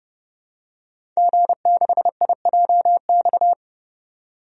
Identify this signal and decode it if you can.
Morse code, heard as a keyed tone, about 31 wpm, 700 Hz. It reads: G6IJX
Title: DXCC-100 Call Signs - 5 Characters - 30wpm